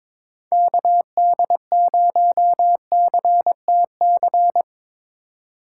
Morse code, heard as a keyed tone, about 24 wpm, 700 Hz. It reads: KD0CTC
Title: Speed Racing - USA 2x3 Call Signs 22wpm